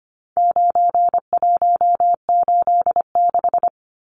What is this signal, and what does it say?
Morse code, heard as a keyed tone, about 25 wpm, 700 Hz. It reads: 9186
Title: Numbers - 4-Digit 25wpm